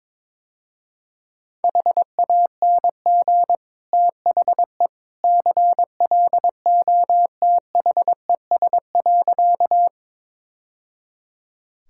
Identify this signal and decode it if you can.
Morse code, heard as a keyed tone, about 22 wpm, 700 Hz. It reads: HANG THE CLOTHES.
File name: practice / 22wpm / Easy Sentences with Hints - 22wpm